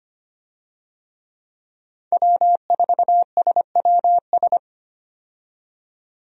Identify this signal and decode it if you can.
Morse code, heard as a keyed tone, about 25 wpm, 700 Hz. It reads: W4SWS
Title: USA Call Signs - 5 Characters - 25wpm